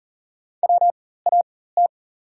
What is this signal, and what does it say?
Morse code, heard as a keyed tone, about 40 wpm, spaced as for 19 wpm, 700 Hz. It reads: WAT
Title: 4x Character Spacing - Top 100 Words - 40wpm